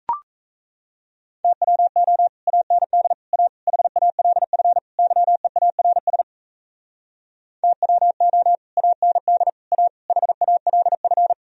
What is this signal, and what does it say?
Morse code, heard as a keyed tone, about 38 wpm, 700 Hz. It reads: TWO AND A HALF YEARS TWO AND A HALF